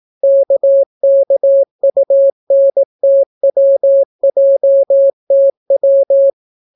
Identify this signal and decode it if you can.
Morse code, heard as a keyed tone, about 18 wpm, 550 Hz. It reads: KKUNTWJTW